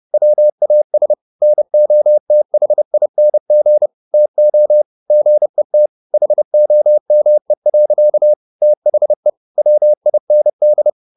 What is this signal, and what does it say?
Morse code, heard as a keyed tone, about 30 wpm, 600 Hz. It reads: WAS NOTHING TO GET HOME. THE WIND